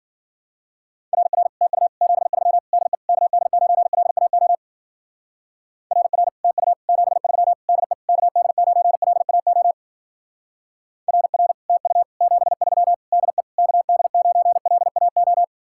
Morse code, heard as a keyed tone, about 50 wpm, 700 Hz. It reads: RR TU 73 DE KD9LAO RR TU 73 DE KD9LAO RR TU 73 DE KD9LAO
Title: Speed-Racing Common QSO Phrases Encore - 40wpm